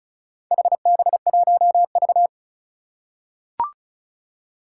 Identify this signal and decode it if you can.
Morse code, heard as a keyed tone, about 36 wpm, 700 Hz. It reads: HB1V
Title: Four Challenging Letter-Number Combinations 35wpm